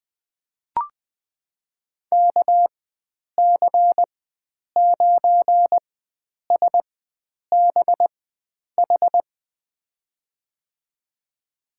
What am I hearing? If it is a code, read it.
Morse code, heard as a keyed tone, about 21 wpm, spaced as for 9 wpm, 700 Hz. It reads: KC9SBH
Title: USA Call Signs - 6 Characters - 4x Character Spacing - 20wpm